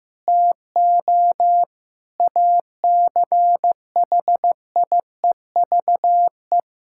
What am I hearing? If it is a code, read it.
Morse code, heard as a keyed tone, about 15 wpm, 700 Hz. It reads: TO ACHIEVE
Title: Sentences from Top 1000 Words - Review Part 6 - 15wpm